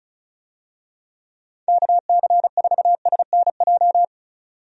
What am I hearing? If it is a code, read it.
Morse code, heard as a keyed tone, about 34 wpm, 700 Hz. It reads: KC4SNJ
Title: USA Call Signs - 6 Characters - 35wpm